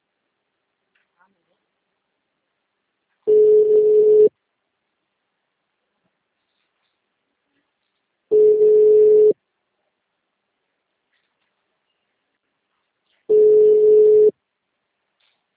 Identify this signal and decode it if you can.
Ringback tone (Europe, 425 Hz), with speech.